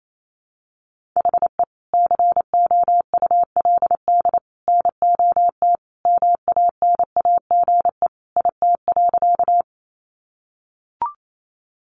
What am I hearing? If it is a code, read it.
Morse code, heard as a keyed tone, about 28 wpm, 700 Hz. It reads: HE COULD NOT MANAGE IT.
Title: Sentences from Top 1400 Words 28wpm